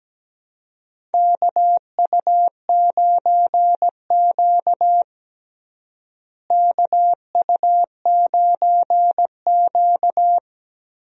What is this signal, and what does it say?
Morse code, heard as a keyed tone, about 17 wpm, 700 Hz. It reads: KU9Q KU9Q
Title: USA Call Signs - 4 Characters - Repeated 2x - 17wpm